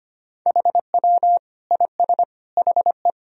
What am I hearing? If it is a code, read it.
Morse code, heard as a keyed tone, about 25 wpm, 700 Hz. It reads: HW IS HE
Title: Sentences from Top 100 Words and Abbreviations 25wpm